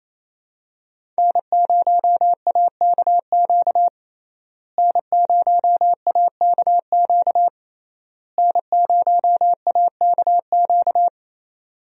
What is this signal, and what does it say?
Morse code, heard as a keyed tone, about 28 wpm, 700 Hz. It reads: N0AKQ N0AKQ N0AKQ
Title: USA Call Signs - 5 Characters - Repeated 3x - 28wpm